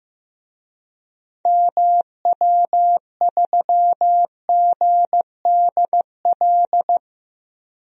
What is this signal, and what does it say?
Morse code, heard as a keyed tone, about 15 wpm, 700 Hz. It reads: MW3GDL
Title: DXCC-100 Call Signs - 6 Characters - 15wpm